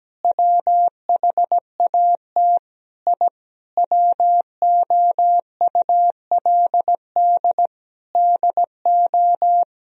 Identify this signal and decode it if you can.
Morse code, heard as a keyed tone, about 17 wpm, 700 Hz. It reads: WHAT I WOULD DO
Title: Sentences from Top 1000 Words - Review Part 3 - 17wpm